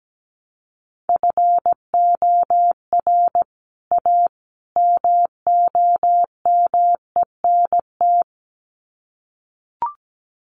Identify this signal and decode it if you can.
Morse code, heard as a keyed tone, about 17 wpm, 700 Hz. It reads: FOR A MOMENT
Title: Sets of 3 Words 17wpm